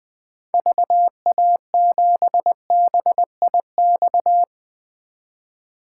Morse code, heard as a keyed tone, about 20 wpm, 700 Hz. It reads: VA7BIX
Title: 2023 POTA Activator Call Signs - 20wpm